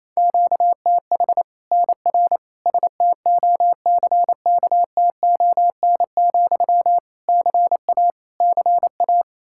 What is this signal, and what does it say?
Morse code, heard as a keyed tone, about 28 wpm, 700 Hz. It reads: QTH NR STOCKTON, CA CA
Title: Practice QSO - Exchange 1 - Beginning - 28wpm